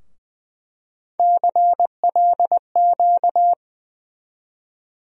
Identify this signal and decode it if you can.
Morse code, heard as a keyed tone, about 20 wpm, 700 Hz. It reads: CLQ
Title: USA Call Sign Postfixes - 3 Letters 20wpm